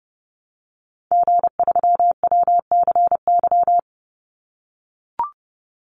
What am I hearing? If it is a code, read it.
Morse code, heard as a keyed tone, about 28 wpm, 700 Hz. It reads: G3WCY
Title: Call Signs - 30wpm